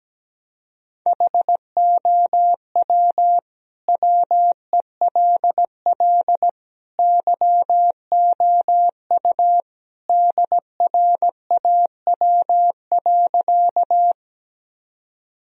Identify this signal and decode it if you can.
Morse code, heard as a keyed tone, about 17 wpm, 700 Hz. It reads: HOW WELL YOU DRAW.
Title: Sentences from Top 1000 Words - Review Part 1 - 17wpm